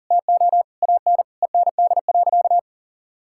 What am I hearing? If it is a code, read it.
Morse code, heard as a keyed tone, about 40 wpm, 700 Hz. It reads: TO AN END.
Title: Sentences from Top 1000 Words - Review Part 5 - 40wpm